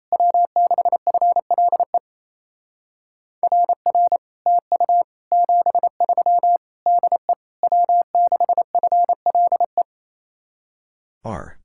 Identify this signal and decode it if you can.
Morse code, heard as a keyed tone, about 36 wpm, 700 Hz. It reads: W6FLE RR TU 73 DE W6FLE